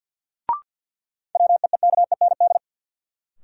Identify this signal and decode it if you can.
Morse code, heard as a keyed tone, about 52 wpm, 700 Hz. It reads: WEEKEND
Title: Sets of 5 Words - New Word Drill - 50wpm